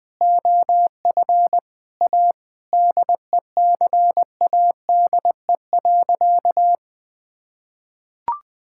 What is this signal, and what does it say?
Morse code, heard as a keyed tone, about 20 wpm, 700 Hz. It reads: OF A DECADE.
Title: Sentences with 6-Word Phrases - 20wpm